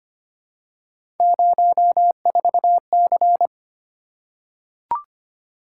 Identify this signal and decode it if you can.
Morse code, heard as a keyed tone, about 25 wpm, 700 Hz. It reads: 04C
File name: Three Letter-Number Combinations - 25wpm